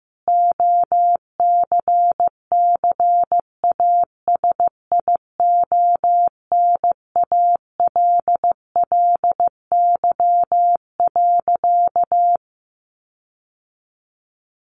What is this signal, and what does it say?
Morse code, heard as a keyed tone, about 15 wpm, 700 Hz. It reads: OCCASIONALLY.
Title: Sentences from Top 1300 Words 15wpm